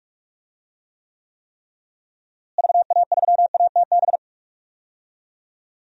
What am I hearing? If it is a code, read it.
Morse code, heard as a keyed tone, about 46 wpm, 700 Hz. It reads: VA3ATB
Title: DXCC-100 Call Signs - 6 Characters - Repeated 3x - 45wpm